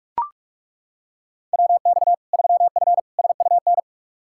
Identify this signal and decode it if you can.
Morse code, heard as a keyed tone, about 42 wpm, 700 Hz. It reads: WX 3F SUN
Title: Practice QSO - Exchange 2 - Info - 45wpm